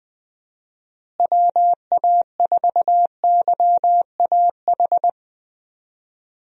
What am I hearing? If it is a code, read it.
Morse code, heard as a keyed tone, about 20 wpm, 700 Hz. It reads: WA4YAH
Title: USA Call Signs - 6 Characters - Repeated 2x - 20wpm